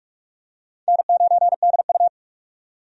Morse code, heard as a keyed tone, about 47 wpm, 700 Hz. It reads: N9DU
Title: USA Call Signs - 4 Characters - 45wpm